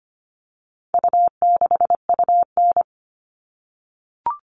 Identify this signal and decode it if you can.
Morse code, heard as a keyed tone, about 25 wpm, 700 Hz. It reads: U6UN
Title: All Cut Numbers - 4-Digit 25wpm